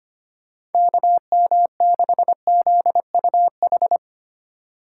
Morse code, heard as a keyed tone, about 25 wpm, 700 Hz. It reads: KM6ZUH
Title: USA Call Signs - 6 Characters - 3x Character Spacing - 25wpm